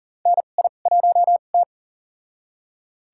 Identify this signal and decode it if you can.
Morse code, heard as a keyed tone, about 40 wpm, spaced as for 29 wpm, 700 Hz. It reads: NI1T
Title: USA Call Signs - 4 Characters - 2x Character Spacing - 40wpm